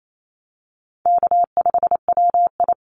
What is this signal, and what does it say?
Morse code, heard as a keyed tone, about 28 wpm, 700 Hz. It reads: K5WI
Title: NA SOTA Chaser Callsigns 28wpm